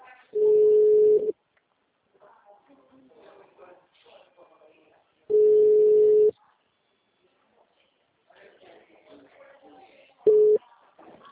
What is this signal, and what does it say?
Ringback tone (Europe, 425 Hz), with speech.